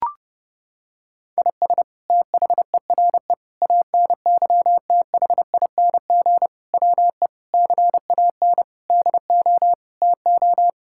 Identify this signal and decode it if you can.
Morse code, heard as a keyed tone, about 30 wpm, 700 Hz. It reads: IS THERE ANYTHING WE CAN DO TO
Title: Sentences from Top 1000 Words - Review Part 6 - 30wpm